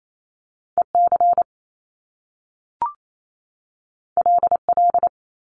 Morse code, heard as a keyed tone, about 28 wpm, 700 Hz. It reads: EC LL
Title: Top Two Letter Combinations in Words 28wpm